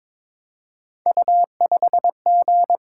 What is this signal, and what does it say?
Morse code, heard as a keyed tone, about 22 wpm, 700 Hz. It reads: U5G
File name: practice / 22wpm / Three Letter-Number Combinations - 22wpm